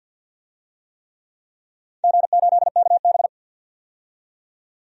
Morse code, heard as a keyed tone, about 47 wpm, 700 Hz. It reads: G8KB
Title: DXCC-100 Call Signs - 4 Characters - 50wpm